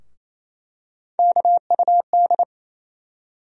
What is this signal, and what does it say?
Morse code, heard as a keyed tone, about 28 wpm, 700 Hz. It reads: KUD
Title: USA Call Sign Postfixes - 3 Letters 28wpm